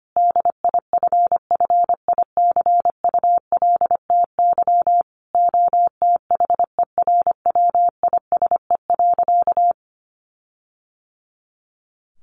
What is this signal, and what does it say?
Morse code, heard as a keyed tone, about 25 wpm, 700 Hz. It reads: DIFFICULTY OTHERWISE.
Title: Sentences from Top 1500 Words 25wpm